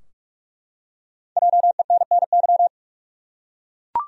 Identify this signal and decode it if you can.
Morse code, heard as a keyed tone, about 43 wpm, 700 Hz. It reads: JENNY
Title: USA Common Names - Max 5 Letters - 45wpm